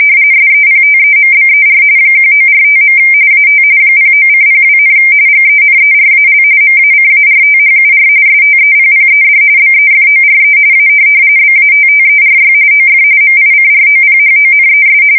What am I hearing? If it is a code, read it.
RTTY, about 45.45 baud. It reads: SEPTEMBER 1945. AT 09:02, GENERAL MACARTHUR STEPPED BEFORE A BATTERY OF MICROPHONES
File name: fsk170 / USS-Missouri